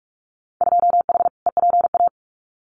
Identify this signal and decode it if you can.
Morse code, heard as a keyed tone, about 44 wpm, 700 Hz. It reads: 2H EPA
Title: Winter Field Day Exchanges Encore 45wpm